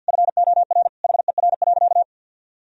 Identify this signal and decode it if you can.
Morse code, heard as a keyed tone, about 50 wpm, 700 Hz. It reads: FOR HER.
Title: Sentences from Top 1000 Words - Review Part 2 - 50wpm